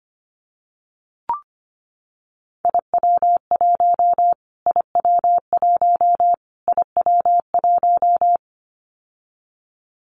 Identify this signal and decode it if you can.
Morse code, heard as a keyed tone, about 25 wpm, 700 Hz. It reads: IW1 IW1 IW1
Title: Callsign Prefixes for Top 25 DXCC - Repeated 3x - 25wpm